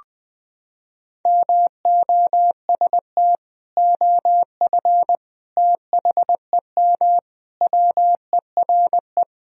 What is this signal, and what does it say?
Morse code, heard as a keyed tone, about 20 wpm, 700 Hz. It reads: MOST OF THEM WERE
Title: Sentences from Top 1000 Words - Review Part 1 - 20wpm